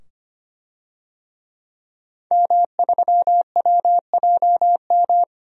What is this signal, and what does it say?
Morse code, heard as a keyed tone, about 25 wpm, 700 Hz. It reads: M3WJM
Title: DXCC-100 Call Signs - 5 Characters - 25wpm